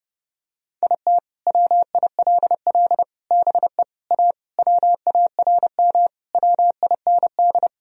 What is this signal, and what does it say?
Morse code, heard as a keyed tone, about 29 wpm, 700 Hz. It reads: IT WILL BE A WARM WIND
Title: Sentences from Top 1000 Words - Review Part 3 - 30wpm